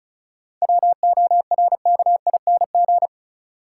Morse code, heard as a keyed tone, about 36 wpm, 700 Hz. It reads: WORKING
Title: Sentences from Top 700 Words - New Word Drill - 35wpm